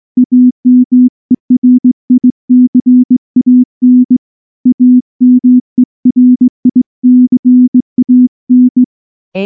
Morse code, heard as a keyed tone, about 20 wpm, 250 Hz. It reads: AMERICAN AMERICAN
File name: morse / morse-training-20WPM-2